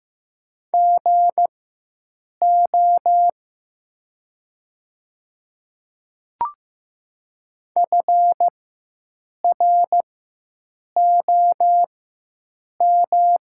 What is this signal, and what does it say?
Morse code, heard as a keyed tone, about 15 wpm, spaced as for 7 wpm, 700 Hz. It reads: GO FROM
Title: 4x Character Spacing - Top 200 Words - 15wpm